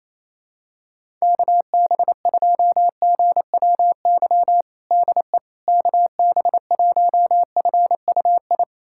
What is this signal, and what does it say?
Morse code, heard as a keyed tone, about 28 wpm, 700 Hz. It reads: KB2GWY DE KB1FUI
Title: USA Callsign DE Callsign - 6 Characters - 28wpm